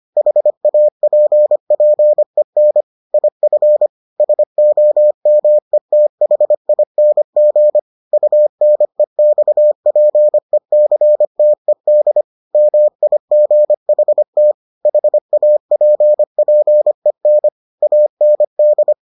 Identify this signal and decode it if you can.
Morse code, heard as a keyed tone, about 25 wpm, 600 Hz. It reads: HAPPEN IF SOMETHING UNEXPECTED MIGHT HAPPEN AND